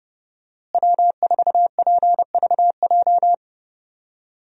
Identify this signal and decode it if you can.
Morse code, heard as a keyed tone, about 30 wpm, 700 Hz. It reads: W4PVJ
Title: USA Call Signs - 5 Characters - Repeated 2x - 30wpm